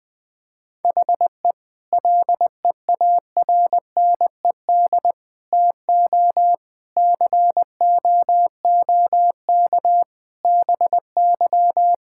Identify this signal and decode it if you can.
Morse code, heard as a keyed tone, about 20 wpm, 700 Hz. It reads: HE LEARNED TO COOK BY
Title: Difficult Sentences with Hints - 20wpm